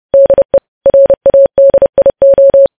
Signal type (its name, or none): Morse code